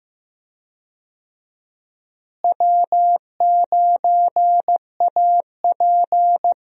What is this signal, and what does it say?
Morse code, heard as a keyed tone, about 15 wpm, 700 Hz. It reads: W9AP
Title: USA Call Signs - 4 Characters - Repeated 3x - 15wpm